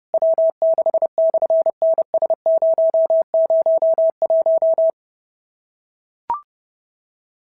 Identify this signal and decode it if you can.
Morse code, heard as a keyed tone, about 29 wpm, 650 Hz. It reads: W6/NS001
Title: SOTA QSO Practice 30wpm